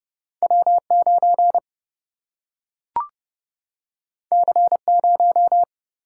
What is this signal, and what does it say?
Morse code, heard as a keyed tone, about 30 wpm, 700 Hz. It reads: W9 C0
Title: Single-Letter Single-Number 30wpm